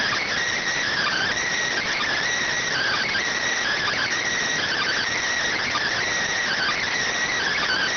Emergency Alert System (SAME) tones (header bursts), heard as harsh buzzing bursts. It is heard from 0.9 to 6.9 s.